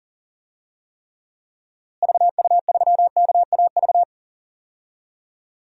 Morse code, heard as a keyed tone, about 40 wpm, 700 Hz. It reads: VU3KAV
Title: DXCC-100 Call Signs - 6 Characters - Repeated 3x - 40wpm